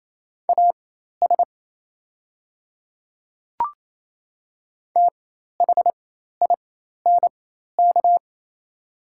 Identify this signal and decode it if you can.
Morse code, heard as a keyed tone, about 28 wpm, spaced as for 13 wpm, 700 Hz. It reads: AS THINK
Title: 4x Character Spacing - Top 300 Words - 28wpm